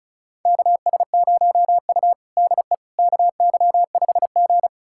Morse code, heard as a keyed tone, about 34 wpm, 700 Hz. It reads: KS0U DE KY5G
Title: USA Callsign DE Callsign - 4 Characters - 35wpm